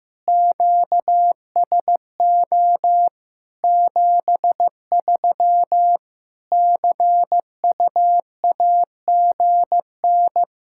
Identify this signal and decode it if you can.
Morse code, heard as a keyed tone, about 15 wpm, 700 Hz. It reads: QSO 73 CUAGN
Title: Practice QSO - Exchange 4 - Ending - 15wpm